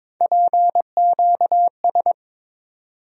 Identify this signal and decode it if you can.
Morse code, heard as a keyed tone, about 22 wpm, 700 Hz. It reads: PQS
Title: USA Call Sign Postfixes - 3 Letters 22wpm